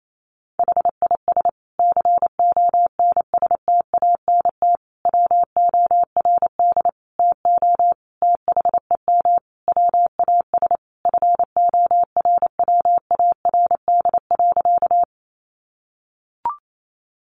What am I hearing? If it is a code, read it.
Morse code, heard as a keyed tone, about 28 wpm, 700 Hz. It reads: HIS CONSTANT WORD TO THEM WAS FORWARD.
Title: Sentences from Top 1500 Words 28wpm